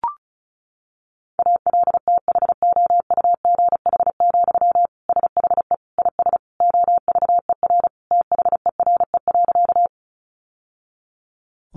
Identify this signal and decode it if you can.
Morse code, heard as a keyed tone, about 34 wpm, 700 Hz. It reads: ALTHOUGH, SHE IS OVER THERE.
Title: Sentences from Top 600 Words 35wpm